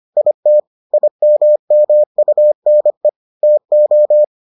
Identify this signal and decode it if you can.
Morse code, heard as a keyed tone, about 25 wpm, 600 Hz. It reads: IT IMMUNE TO